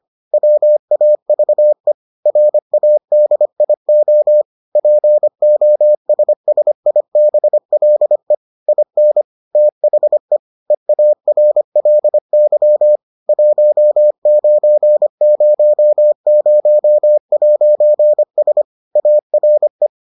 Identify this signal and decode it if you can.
Morse code, heard as a keyed tone, about 25 wpm, 600 Hz. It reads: WAVE RADIO POSSIBLE IN THE EARLY 1900'S ARE